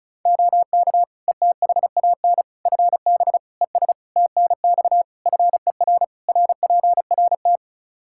Morse code, heard as a keyed tone, about 36 wpm, 700 Hz. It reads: OK ETHAN FB ES TNX FER RPRT
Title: Practice QSO - Exchange 2 - Info - 35wpm